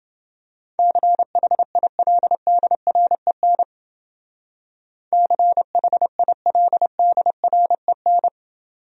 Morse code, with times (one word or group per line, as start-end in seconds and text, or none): 0.8-3.6 CHILDREN
5.1-8.3 CHILDREN